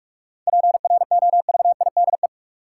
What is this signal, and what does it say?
Morse code, heard as a keyed tone, about 44 wpm, 700 Hz. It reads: PROVIDE
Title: Sentences from Top 1000 Words - Review Part 2 - New Word Drill - 45wpm